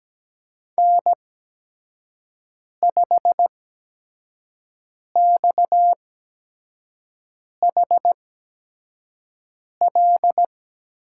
Morse code, heard as a keyed tone, about 17 wpm, spaced as for 5 wpm, 700 Hz. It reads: N5XHL
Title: USA Call Signs - 8x Character Spacing - 17wpm